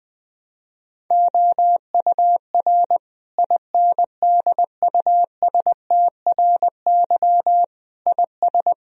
Morse code, heard as a keyed tone, about 20 wpm, 700 Hz. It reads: OUR INDUSTRY IS
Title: Sentences from Top 1000 Words - Review Part 4 - 20wpm